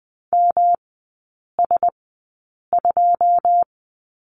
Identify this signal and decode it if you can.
Morse code, heard as a keyed tone, about 20 wpm, spaced as for 8 wpm, 700 Hz. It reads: MS2
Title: 178 Lesson 29e - Slash - Three Characters 20wpm